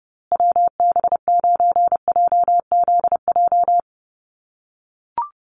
Morse code, heard as a keyed tone, about 30 wpm, 700 Hz. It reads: WB9JZJ
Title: USA Call Signs Repeated 3x 30wpm